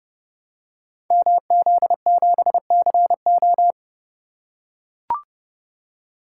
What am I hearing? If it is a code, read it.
Morse code, heard as a keyed tone, about 30 wpm, 700 Hz. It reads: MZ7CO
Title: Code Groups - 5 Letter:Numbers - 30wpm